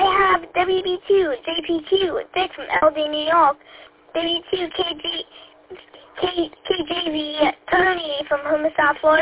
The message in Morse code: none

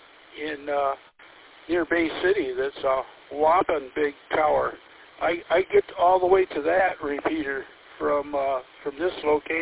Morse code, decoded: none

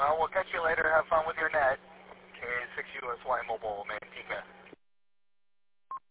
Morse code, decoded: none

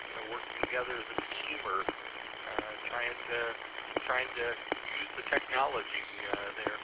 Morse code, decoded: none

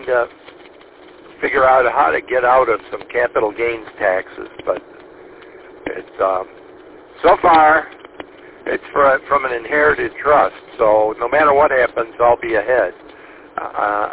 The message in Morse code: none

